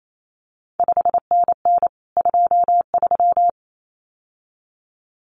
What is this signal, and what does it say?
Morse code, heard as a keyed tone, about 28 wpm, 700 Hz. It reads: 5NN 23
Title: CQ WW DX Contest Exchange Encore - 28wpm